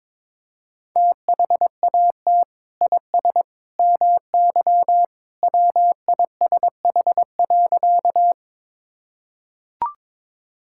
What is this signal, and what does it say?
Morse code, heard as a keyed tone, about 22 wpm, 700 Hz. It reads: THAT IS MY WISH.
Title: Sentences from Top 1600 Words 22wpm